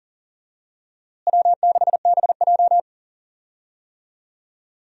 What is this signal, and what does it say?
Morse code, heard as a keyed tone, about 40 wpm, 700 Hz. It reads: W6BJ
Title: USA Call Signs - 4 Characters - Repeated 2x - 40wpm